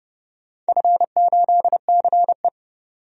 Morse code, heard as a keyed tone, about 30 wpm, 700 Hz. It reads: F8CE
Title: DXCC-100 Call Signs - 4 Characters - 30wpm